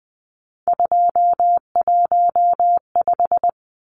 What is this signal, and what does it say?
Morse code, heard as a keyed tone, about 20 wpm, 700 Hz. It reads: 215
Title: Numbers - 3-Digit 20wpm